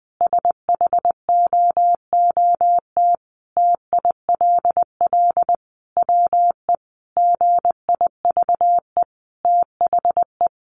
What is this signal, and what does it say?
Morse code, heard as a keyed tone, about 20 wpm, 700 Hz. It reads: SHOOT TILL WE GIVE THE
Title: A Sound Of Thunder 20wpm